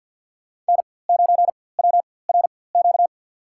Morse code, heard as a keyed tone, about 48 wpm, spaced as for 23 wpm, 700 Hz. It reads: N9WRQ
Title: USA Call Signs - 5 Characters - 4x Character Spacing - 50wpm